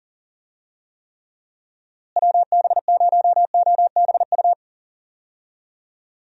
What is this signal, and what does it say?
Morse code, heard as a keyed tone, about 40 wpm, 700 Hz. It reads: WB0OBU
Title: USA Call Signs - 6 Characters - Repeated 3x - 40wpm